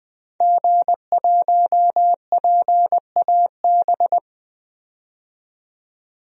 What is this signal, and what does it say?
Morse code, heard as a keyed tone, about 20 wpm, 700 Hz. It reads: G1PAB
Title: DXCC-100 Call Signs - 5 Characters - 20wpm